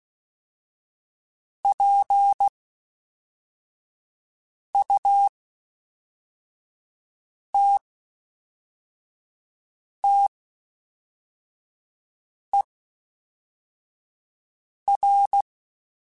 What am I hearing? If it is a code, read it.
Morse code, heard as a keyed tone, about 16 wpm, spaced as for 4 wpm, 800 Hz. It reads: PUTTER
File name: Special INGs / Special INGs-mp3s / Fast ing @10